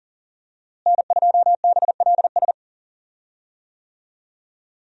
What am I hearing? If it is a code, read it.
Morse code, heard as a keyed tone, about 40 wpm, 700 Hz. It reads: N2BLS
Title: USA Call Signs - 5 Characters - Repeated 3x - 40wpm